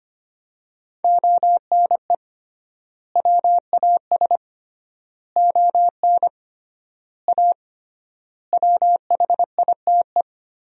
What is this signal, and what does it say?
Morse code, heard as a keyed tone, about 25 wpm, 700 Hz. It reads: ONE WAS ON A WHITE
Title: 3x Word-Spacing - Sentences from Top 300 Words - 25wpm